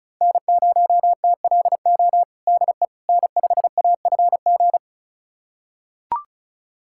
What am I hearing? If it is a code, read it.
Morse code, heard as a keyed tone, about 34 wpm, 700 Hz. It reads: N0TLO DE N5AFG
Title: USA Callsign DE Callsign - 5 Characters - 35wpm